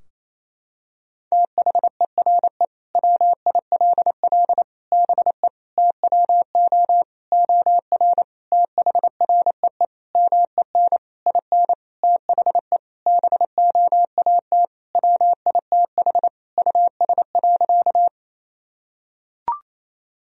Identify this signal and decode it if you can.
Morse code, heard as a keyed tone, about 28 wpm, 700 Hz. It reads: THERE WILL BE TWO OR THREE MEN IN THE BOAT WITH US.
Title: Sentences from Top 1000 Words - Review Part 3 - 28wpm